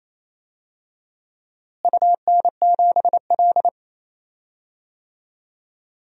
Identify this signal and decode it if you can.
Morse code, heard as a keyed tone, about 28 wpm, 700 Hz. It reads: UN7L
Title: DXCC-100 Call Signs - 4 Characters - Repeated 2x - 28wpm